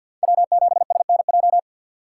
Morse code, heard as a keyed tone, about 52 wpm, 700 Hz. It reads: W7SNJ
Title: USA Call Signs - 5 Characters - 4x Character Spacing - 50wpm